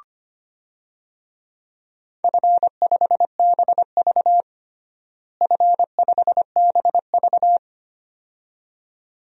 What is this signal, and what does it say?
Morse code, heard as a keyed tone, about 25 wpm, 700 Hz. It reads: F5BV F5BV
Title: DXCC-100 Call Signs - 4 Characters - Repeated 2x - 25wpm